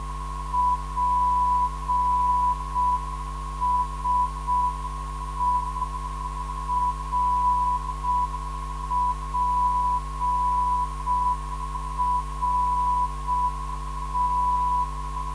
Morse code, heard as a keyed tone, about 6 wpm, 990 Hz. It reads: PSE RPRT
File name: ndb zvok / DI2BO 440kHz